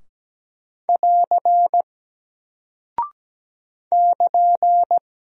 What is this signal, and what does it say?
Morse code, heard as a keyed tone, about 17 wpm, 700 Hz. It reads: <AR> <KN>
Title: Single Character (Letter:Number:Punctuation:Prosign) 17wpm